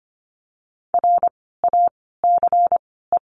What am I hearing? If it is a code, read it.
Morse code, heard as a keyed tone, about 25 wpm, spaced as for 16 wpm, 700 Hz. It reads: RACE
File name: course / 25:10wpm / 094 Lesson 16f - Letter c - Words 25wpm